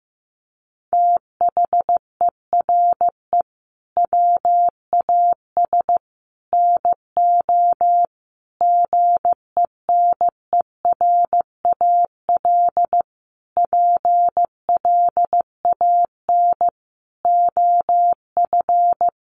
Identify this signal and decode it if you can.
Morse code, heard as a keyed tone, about 15 wpm, 700 Hz. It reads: THERE WAS NO GENERAL PLAN OF
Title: Sentences from Top 700 Words 15wpm